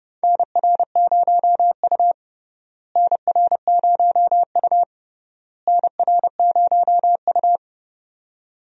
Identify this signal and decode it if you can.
Morse code, heard as a keyed tone, about 30 wpm, 700 Hz. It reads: NR0U NR0U NR0U
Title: USA Call Signs - 4 Characters - Repeated 3x - 30wpm